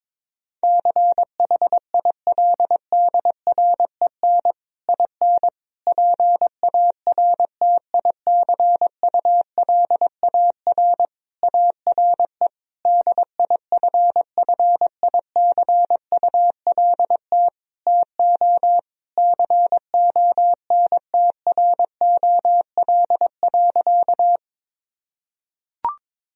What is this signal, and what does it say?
Morse code, heard as a keyed tone, about 21 wpm, 700 Hz. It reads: CHILDREN IN PARTICULAR ARE DIFFICULT TO CONTROL.
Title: Sentences from Top 1000 Words - Review Part 4 - 22wpm